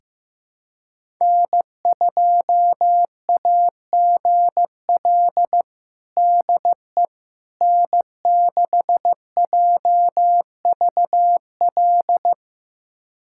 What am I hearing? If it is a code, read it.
Morse code, heard as a keyed tone, about 15 wpm, 700 Hz. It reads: N2AGL DE N6JVL
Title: USA Callsign DE Callsign - 5 Characters - 15wpm